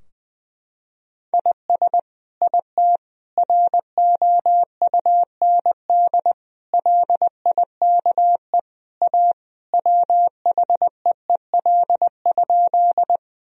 Morse code, heard as a keyed tone, about 20 wpm, 700 Hz. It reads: IS IT ROUND LIKE A WHEEL?
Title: Sentences from Top 1000 Words - Review Part 3 - 20wpm